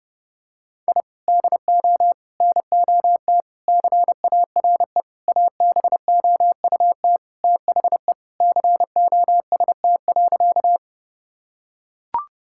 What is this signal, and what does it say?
Morse code, heard as a keyed tone, about 30 wpm, 700 Hz. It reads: I DO NOT CARE ABOUT THE COST.
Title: Sentences from Top 1000 Words - Review Part 5 - 30wpm